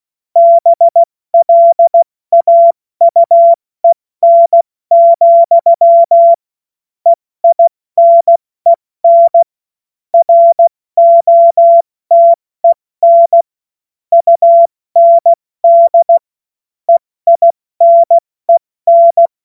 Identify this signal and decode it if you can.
Morse code, heard as a keyed tone, about 16 wpm, 680 Hz. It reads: BLAUEN, EINEN ROTEN UND EINEN
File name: Morsekurs / Witze / Witz 28. September 2021, 70 F80